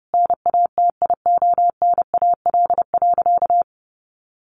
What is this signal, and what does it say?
Morse code, heard as a keyed tone, about 30 wpm, 700 Hz. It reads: NATIONAL.
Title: Sentences from Top 1200 Words 30wpm